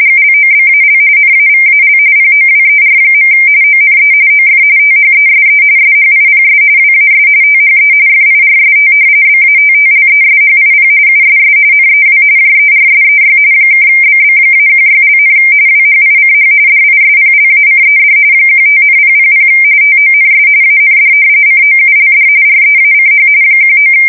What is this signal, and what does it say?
RTTY, about 45.45 baud. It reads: GLOBAL NAVAL POWER. ONE OF THESE SHIPS WAS THE FIRST BATTLESHIP USS NEW JERSEY (BB-16), WHICH DISPLACED 15,000 TONS AND COULD REACH 19
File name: fsk170 / battleshipnj